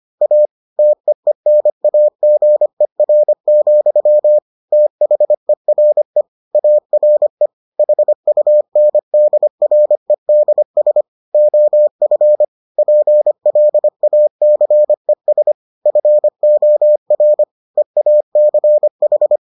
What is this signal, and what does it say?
Morse code, heard as a keyed tone, about 25 wpm, 600 Hz. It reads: A TEENAGER, THERE ARE HUNDREDS OF PLACES FOR EACH